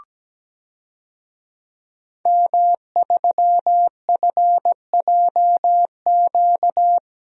Morse code, heard as a keyed tone, about 17 wpm, 700 Hz. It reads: M3FJQ
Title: DXCC-100 Call Signs - 5 Characters - Repeated 2x - 17wpm